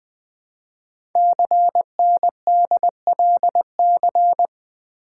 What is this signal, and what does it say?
Morse code, heard as a keyed tone, about 20 wpm, 700 Hz. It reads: CNDLC
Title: Code Groups - 5 Letters - 20wpm